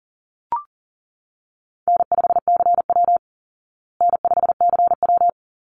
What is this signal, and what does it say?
Morse code, heard as a keyed tone, about 40 wpm, 700 Hz. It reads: N5CW N5CW
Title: Easy Call Signs Repeated 2x - 40wpm